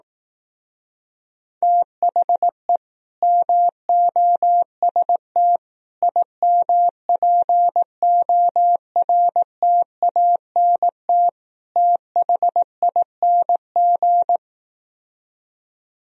Morse code, heard as a keyed tone, about 18 wpm, 700 Hz. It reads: THE MOST IMPORTANT THING
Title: Speed-Racing - Sets of 4 Words 15wpm